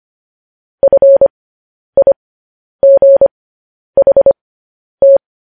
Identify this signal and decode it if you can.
Morse code, heard as a keyed tone, about 25 wpm, spaced as for 10 wpm, 550 Hz. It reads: FIGHT
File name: podcast / News headlines-1x-25x10